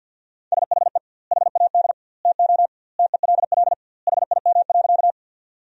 Morse code, heard as a keyed tone, about 50 wpm, 700 Hz. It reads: SHE HAD TO TELL HIM.
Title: 2x Word-Spacing - Sentences from Top 200 Words - 50wpm